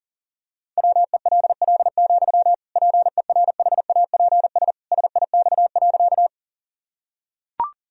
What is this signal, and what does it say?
Morse code, heard as a keyed tone, about 40 wpm, 700 Hz. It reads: WELL, PERHAPS SIX.
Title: Sentences from Top 1000 Words - Review Part 2 - 40wpm